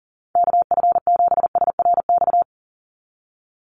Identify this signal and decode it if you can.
Morse code, heard as a keyed tone, about 40 wpm, 690 Hz. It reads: KF7SRX
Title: USA Call Signs 40wpm